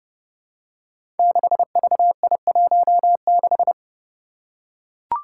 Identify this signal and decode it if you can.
Morse code, heard as a keyed tone, about 30 wpm, 700 Hz. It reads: 6VI16
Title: Five Challenging Letter-Number Combinations 30wpm